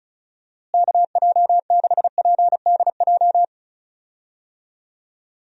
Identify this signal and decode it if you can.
Morse code, heard as a keyed tone, about 36 wpm, 700 Hz. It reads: KJ6PDJ
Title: USA Call Signs - 6 Characters - Repeated 2x - 35wpm